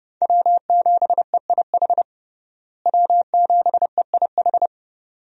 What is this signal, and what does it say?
Morse code, heard as a keyed tone, about 30 wpm, 700 Hz. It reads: W7EIH W7EIH
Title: USA Call Signs - 5 Characters - Repeated 2x - 30wpm